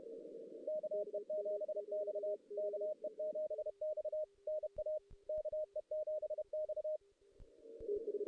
Morse code, heard as a keyed tone, about 30 wpm, 620 Hz. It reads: KE7X KE7X NA KE7X
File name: 2016 JAN CW NAQP / 2016 Jan naqp k5ot - 0-5